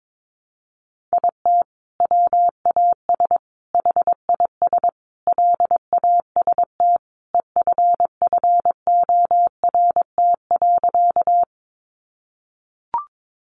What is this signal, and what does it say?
Morse code, heard as a keyed tone, about 22 wpm, 700 Hz. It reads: IT WAS HIS LAST EFFORT.
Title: Sentences from Top 1200 Words 22wpm